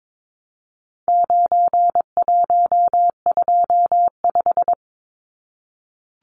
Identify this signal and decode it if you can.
Morse code, heard as a keyed tone, about 22 wpm, 700 Hz. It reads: 9125
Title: Numbers - 4-Digit 22wpm